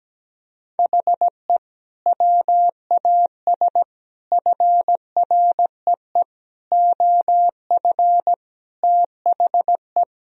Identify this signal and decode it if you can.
Morse code, heard as a keyed tone, about 17 wpm, 700 Hz. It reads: HE WAS FREE OF THE
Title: Sentences from Top 1000 Words - Review Part 4 - 17wpm